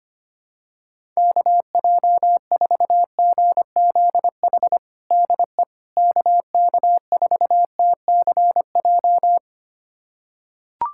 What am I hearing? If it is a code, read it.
Morse code, heard as a keyed tone, about 25 wpm, 700 Hz. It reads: KJ4GZH DE KK4TCJ
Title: USA Callsign DE Callsign - 6 Characters - 25wpm